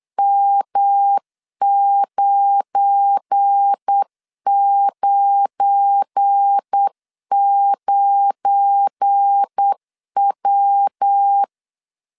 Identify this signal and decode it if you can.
Morse code, heard as a keyed tone, about 8 wpm, 800 Hz. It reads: M999W
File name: LEZIONI / 7 GF9Z / 7 GF9 8wpm 3545